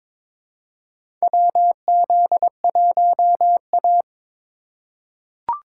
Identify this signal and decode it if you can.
Morse code, heard as a keyed tone, about 22 wpm, 700 Hz. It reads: WZ1A
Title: USA Call Signs - 4 Characters - 2x Character Spacing - 22wpm